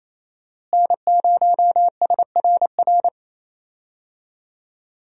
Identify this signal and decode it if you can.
Morse code, heard as a keyed tone, about 28 wpm, 700 Hz. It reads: N0SRR
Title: USA Call Signs - 5 Characters - Repeated 2x - 28wpm